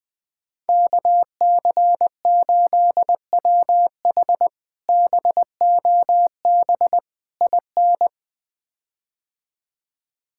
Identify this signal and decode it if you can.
Morse code, heard as a keyed tone, about 20 wpm, 700 Hz. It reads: KC8WH BOB IN
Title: K1USN SST QSO Practice - 20wpm